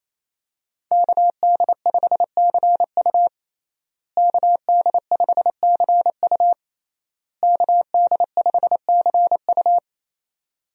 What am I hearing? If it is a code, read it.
Morse code, heard as a keyed tone, about 28 wpm, 700 Hz. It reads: KD5CU KD5CU KD5CU
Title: USA Call Signs - 5 Characters - Repeated 3x - 28wpm